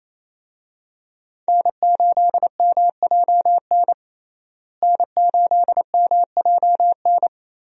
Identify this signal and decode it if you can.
Morse code, heard as a keyed tone, about 27 wpm, 700 Hz. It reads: N8MJN N8MJN
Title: USA Call Signs - 5 Characters - Repeated 3x - 28wpm